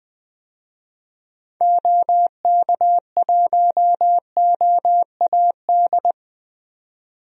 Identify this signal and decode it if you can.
Morse code, heard as a keyed tone, about 20 wpm, 700 Hz. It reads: OK1OAD
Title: DXCC-100 Call Signs - 6 Characters - 20wpm